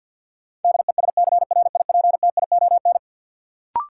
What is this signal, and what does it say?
Morse code, heard as a keyed tone, about 52 wpm, 700 Hz. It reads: DESCRIPTION
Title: Sentences from Top 1300 Words - New Word Drill - 50wpm